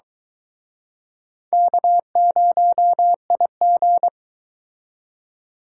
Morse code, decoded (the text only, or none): K0IG